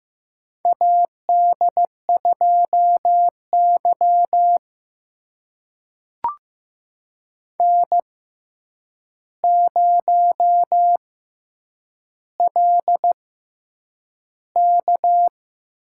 Morse code, heard as a keyed tone, about 15 wpm, spaced as for 8 wpm, 700 Hz. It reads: AD2Y N0LK
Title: USA Call Signs - 4 Characters - 6x Character Spacing - 15wpm